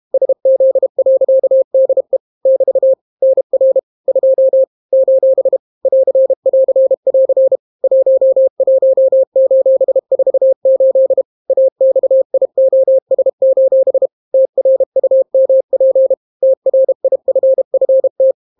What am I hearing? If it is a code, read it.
Morse code, heard as a keyed tone, about 32 wpm, 530 Hz. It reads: SZ.DE <BT> NR 2 : <AR><AR><AR> 11:48 AXIOS: TRUMP TRIFFT